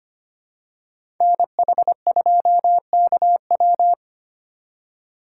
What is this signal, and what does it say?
Morse code, heard as a keyed tone, about 25 wpm, 700 Hz. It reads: NH2KW
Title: USA Call Signs - 5 Characters - 2x Character Spacing - 25wpm